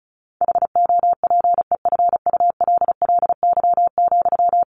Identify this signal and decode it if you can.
Morse code, heard as a keyed tone, about 34 wpm, 700 Hz. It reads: HOPEFULLY,
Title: Sentences from Top 1500 Words 35wpm